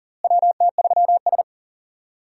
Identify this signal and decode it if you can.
Morse code, heard as a keyed tone, about 40 wpm, 700 Hz. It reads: WT3S
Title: USA Call Signs - 4 Characters - Repeated 3x - 40wpm